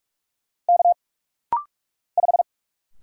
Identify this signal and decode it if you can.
Morse code, heard as a keyed tone, about 44 wpm, 700 Hz. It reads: K 5
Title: Single Letter-Number Alt - Rapid-Fire 45wpm